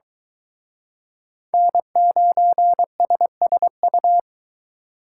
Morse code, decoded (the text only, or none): N9SSU